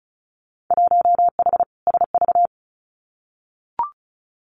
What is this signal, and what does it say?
Morse code, heard as a keyed tone, about 36 wpm, 690 Hz. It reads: 1H SV
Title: Winter Field Day Exchanges 35wpm